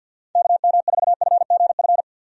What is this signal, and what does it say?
Morse code, heard as a keyed tone, about 50 wpm, 700 Hz. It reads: KM4FGF
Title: USA Call Signs - 6 Characters - 50wpm